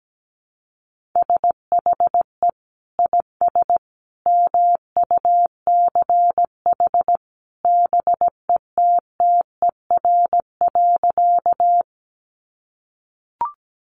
Morse code, heard as a keyed tone, about 17 wpm, 700 Hz. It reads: SHE IS MUCH BETTER.
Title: Sentences from Top 400 Words 17wpm